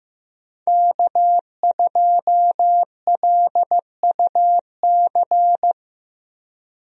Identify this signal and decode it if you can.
Morse code, heard as a keyed tone, about 15 wpm, 700 Hz. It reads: K2LUC
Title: USA Call Signs - 5 Characters - Repeated 3x - 15wpm